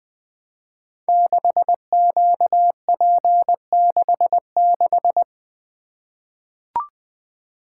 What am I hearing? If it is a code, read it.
Morse code, heard as a keyed tone, about 21 wpm, 700 Hz. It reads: 6QP66
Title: Code Groups - 5 Letter:Numbers - 20wpm